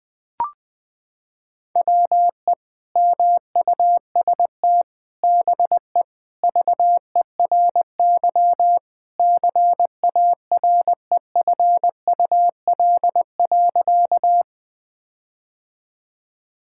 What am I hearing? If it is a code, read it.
Morse code, heard as a keyed tone, about 20 wpm, 700 Hz. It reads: WE MUST BE VERY CAREFUL.
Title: Sentences from Top 1600 Words 20wpm